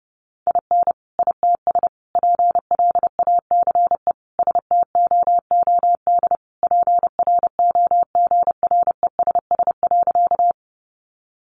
Morse code, heard as a keyed tone, about 30 wpm, 700 Hz. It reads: IN ITS PLACE STOOD PROGRESS.
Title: Sentences from Top 1400 Words 30wpm